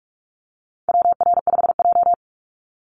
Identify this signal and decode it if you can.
Morse code, heard as a keyed tone, about 44 wpm, 700 Hz. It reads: WR5J
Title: Call Signs Repeated 2x - 45wpm